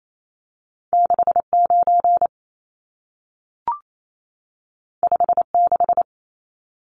Morse code, heard as a keyed tone, about 28 wpm, 700 Hz. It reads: 69 56
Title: Numbers - 2-Digit 28wpm